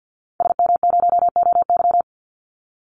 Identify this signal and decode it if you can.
Morse code, heard as a keyed tone, about 50 wpm, 700 Hz. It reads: SM0OY
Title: Call Signs - 50wpm